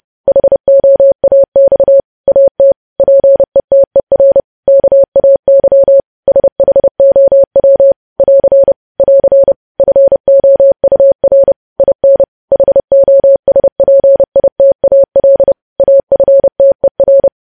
Morse code, heard as a keyed tone, about 30 wpm, 550 Hz. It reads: HOAX AT PETER KAY SHOW <AR> <AR> FOUR IN HOSPITAL AFTER